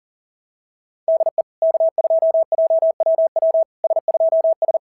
Morse code, heard as a keyed tone, about 40 wpm, 690 Hz. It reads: DE K2JWW S2S
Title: SOTA QSO Practice 40wpm